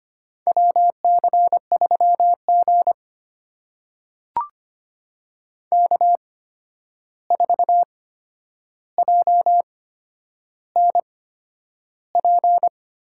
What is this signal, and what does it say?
Morse code, heard as a keyed tone, about 25 wpm, spaced as for 10 wpm, 700 Hz. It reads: WC3G K4JNP
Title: USA Call Signs - 8x Character Spacing - 25wpm